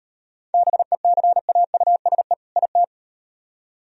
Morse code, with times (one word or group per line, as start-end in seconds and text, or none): 0.5-2.3 BECAUSE
2.6-2.8 IT